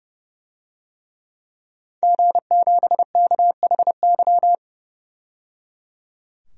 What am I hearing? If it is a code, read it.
Morse code, heard as a keyed tone, about 30 wpm, 700 Hz. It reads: G7KHY
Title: DXCC-100 Call Signs - 5 Characters - 30wpm